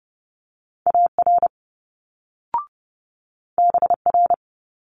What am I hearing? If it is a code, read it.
Morse code, heard as a keyed tone, about 30 wpm, 700 Hz. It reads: AR BR
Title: Multi-letter Phonemes Quick Fire 30wpm